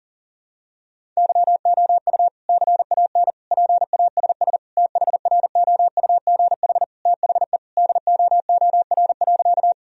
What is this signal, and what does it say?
Morse code, heard as a keyed tone, about 37 wpm, 700 Hz. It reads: YOU CAN PASS THROUGH THE DOOR.
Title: Sentences from Top 1000 Words - Review Part 1 - 40wpm